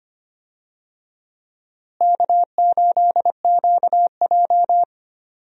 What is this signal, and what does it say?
Morse code, heard as a keyed tone, about 25 wpm, 700 Hz. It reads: K8QJ
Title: USA Call Signs - 4 Characters - Repeated 3x - 25wpm